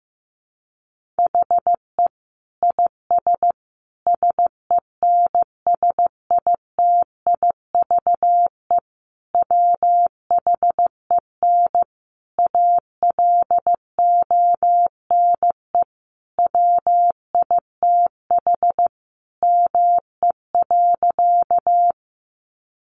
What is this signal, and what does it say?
Morse code, heard as a keyed tone, about 15 wpm, 700 Hz. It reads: HE IS SENSITIVE WHEN ALONE WITH ME.
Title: Sentences from Top 1400 Words 15wpm